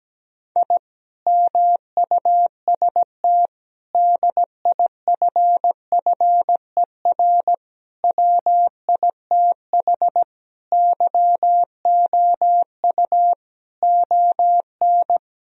Morse code, heard as a keyed tone, about 17 wpm, 700 Hz. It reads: I MUST DIFFER WITH YOU ON
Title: Sentences from Top 1000 Words - Review Part 1 - 17wpm